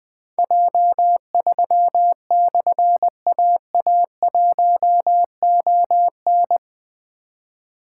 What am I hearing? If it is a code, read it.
Morse code, heard as a keyed tone, about 20 wpm, 700 Hz. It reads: J3/AA1ON
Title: Difficult Call Signs 20wpm